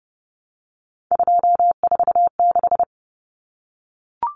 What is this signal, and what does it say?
Morse code, heard as a keyed tone, about 30 wpm, 700 Hz. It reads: 246
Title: Common Cut Numbers - 3-Digit 30wpm